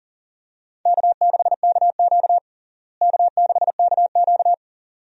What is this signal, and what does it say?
Morse code, heard as a keyed tone, about 40 wpm, 700 Hz. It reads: K6KQ K6KQ
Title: USA Call Signs - 4 Characters - Repeated 2x - 40wpm